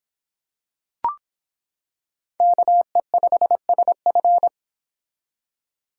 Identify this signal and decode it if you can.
Morse code, heard as a keyed tone, about 26 wpm, 700 Hz. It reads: KE5SF